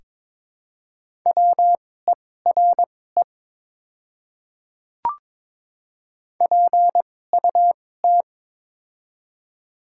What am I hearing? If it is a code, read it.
Morse code, heard as a keyed tone, about 21 wpm, spaced as for 16 wpm, 700 Hz. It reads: WERE PUT
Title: 2x Character Spacing - Top 200 Words - 22wpm